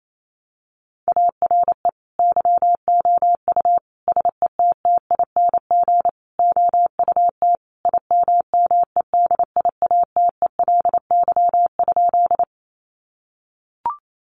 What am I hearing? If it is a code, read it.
Morse code, heard as a keyed tone, about 28 wpm, 700 Hz. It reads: ARE YOU SETTING OUT IMMEDIATELY?
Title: Sentences from Top 1300 Words 28wpm